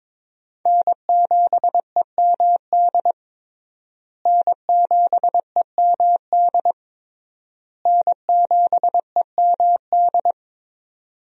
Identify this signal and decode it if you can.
Morse code, heard as a keyed tone, about 21 wpm, 700 Hz. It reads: N7EMD N7EMD N7EMD
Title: USA Call Signs - 5 Characters - Repeated 3x - 22wpm